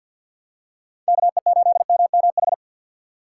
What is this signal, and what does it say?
Morse code, heard as a keyed tone, about 46 wpm, 700 Hz. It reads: KE8MMH
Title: USA Call Signs - 6 Characters - 4x Character Spacing - 50wpm